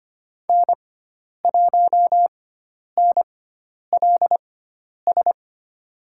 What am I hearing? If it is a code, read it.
Morse code, heard as a keyed tone, about 25 wpm, spaced as for 10 wpm, 700 Hz. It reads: N1NLS
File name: Lesson 16g - Letter c - Call Signs 25wpm